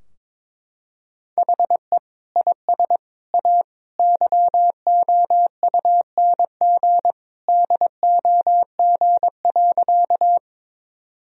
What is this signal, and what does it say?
Morse code, heard as a keyed tone, about 22 wpm, 700 Hz. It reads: HE IS A YOUNG DOG.
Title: Sentences from Top 1000 Words - Review Part 2 - 22wpm